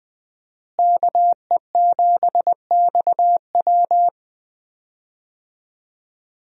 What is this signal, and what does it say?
Morse code, heard as a keyed tone, about 20 wpm, 700 Hz. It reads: KE7XW
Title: USA Call Signs - 5 Characters - Repeated 2x - 20wpm